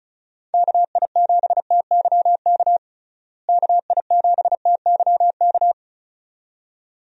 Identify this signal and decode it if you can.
Morse code, heard as a keyed tone, about 36 wpm, 700 Hz. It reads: KI7TYK KI7TYK
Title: USA Call Signs - 6 Characters - Repeated 2x - 35wpm